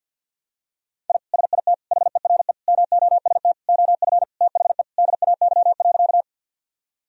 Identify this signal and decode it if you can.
Morse code, heard as a keyed tone, about 46 wpm, 700 Hz. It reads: I SIT HERE MOST OF THE DAY.
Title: Sentences from Top 1000 Words - Review Part 1 - 50wpm